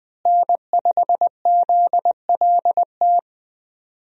Morse code, heard as a keyed tone, about 20 wpm, 700 Hz. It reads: N5ZLT
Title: USA Call Signs - 5 Characters - Repeated 2x - 20wpm